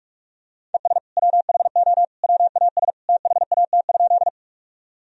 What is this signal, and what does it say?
Morse code, heard as a keyed tone, about 47 wpm, 700 Hz. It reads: ES WHO WAS THAT?
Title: Sentences from Top 100 Words and Abbreviations 45wpm